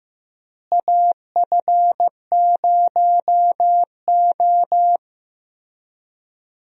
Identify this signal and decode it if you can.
Morse code, heard as a keyed tone, about 16 wpm, 700 Hz. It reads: AF0O
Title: USA Call Signs - 4 Characters - Repeated 3x - 15wpm